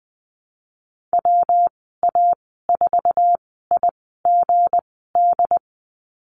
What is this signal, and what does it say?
Morse code, heard as a keyed tone, about 20 wpm, spaced as for 15 wpm, 700 Hz. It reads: WA4IGD
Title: 213 Lesson 34g - Number 8 - Call Signs 20wpm